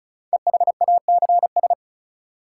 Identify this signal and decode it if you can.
Morse code, heard as a keyed tone, about 36 wpm, 700 Hz. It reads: EHACS
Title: Code Groups - 5 Letters - 35wpm